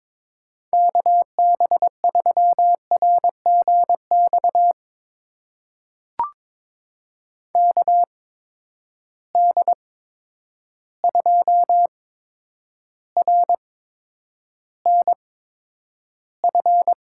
Morse code, heard as a keyed tone, about 21 wpm, spaced as for 9 wpm, 700 Hz. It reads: KB3RGX KD2RNF
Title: USA Call Signs - 6 Characters - 8x Character Spacing - 22wpm